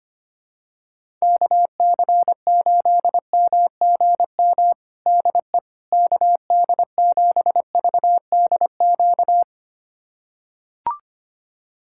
Morse code, heard as a keyed tone, about 25 wpm, 700 Hz. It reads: KC8MGM DE KD7VDQ
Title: USA Callsign DE Callsign - 6 Characters - 25wpm